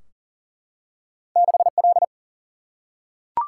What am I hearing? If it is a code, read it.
Morse code, heard as a keyed tone, about 38 wpm, 700 Hz. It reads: 6L
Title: Single-Letter Single-Number 40wpm